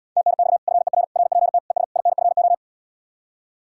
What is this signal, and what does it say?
Morse code, heard as a keyed tone, about 80 wpm, 700 Hz. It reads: TNX FER RPT ES INFO